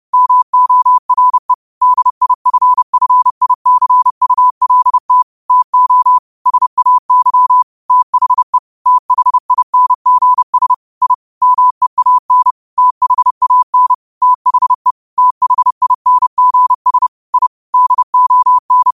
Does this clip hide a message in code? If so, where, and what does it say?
Yes: Morse code from 0.1 to 18.9 s, heard as a keyed tone, about 30 wpm, 1000 Hz. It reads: MORE DIFFICULT TO SAY THE THINGS I MEAN THAN THE THINGS I DON